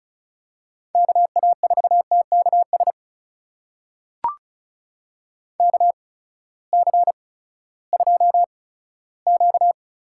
Morse code, heard as a keyed tone, about 36 wpm, spaced as for 18 wpm, 700 Hz. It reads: KA4TKS KC2Q
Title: USA Call Signs - 6 Characters - 8x Character Spacing - 35wpm